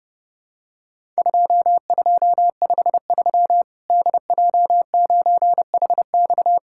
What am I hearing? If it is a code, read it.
Morse code, heard as a keyed tone, about 30 wpm, 700 Hz. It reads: 2253 DJ9HX
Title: Worked All Europe DX Contest - QTC Exchange - 30wpm